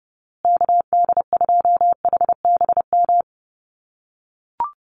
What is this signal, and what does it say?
Morse code, heard as a keyed tone, about 30 wpm, 700 Hz. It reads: KD2HBM
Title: USA Call Signs Repeated 3x 30wpm